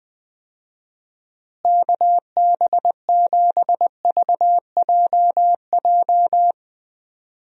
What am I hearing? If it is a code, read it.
Morse code, heard as a keyed tone, about 20 wpm, 700 Hz. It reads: KB7VJJ
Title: USA Call Signs - 6 Characters - Repeated 3x - 20wpm